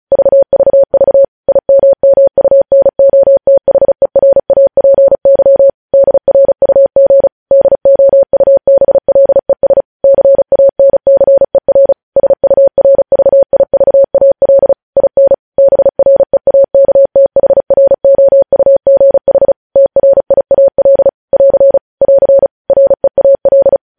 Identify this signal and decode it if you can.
Morse code, heard as a keyed tone, about 36 wpm, 550 Hz. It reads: VVV IMMUNOTHERAPY DRUG DOUBLES CANCER SURVIVAL IN BREAKTHROUGH TRIAL <AR> <AR> REAL
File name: podcast / News headlines-1x-35x35